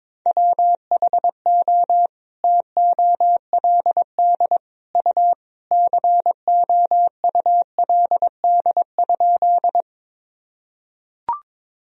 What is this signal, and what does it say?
Morse code, heard as a keyed tone, about 22 wpm, 700 Hz. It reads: WHO TOLD U COULD?
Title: Sentences from Top 400 Words and Abbreviations 22wpm